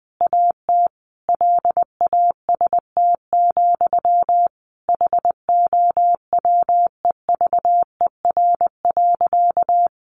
Morse code, heard as a keyed tone, about 20 wpm, 700 Hz. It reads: AT LAST, HOWEVER.
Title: Sentences from Top 600 Words 20wpm